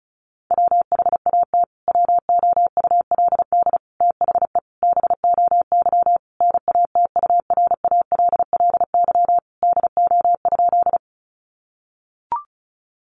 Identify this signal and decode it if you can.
Morse code, heard as a keyed tone, about 34 wpm, 700 Hz. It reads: WHAT WOULD THE BOY NATURALLY DO?
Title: Sentences from Top 1300 Words 35wpm